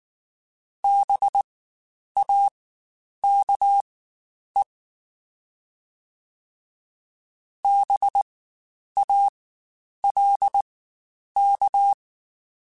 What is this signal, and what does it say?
Morse code, heard as a keyed tone, about 19 wpm, spaced as for 9 wpm, 800 Hz. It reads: BAKE BALK
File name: Spaced Alphas / Spaced Alphas mp3s / Bs sp @18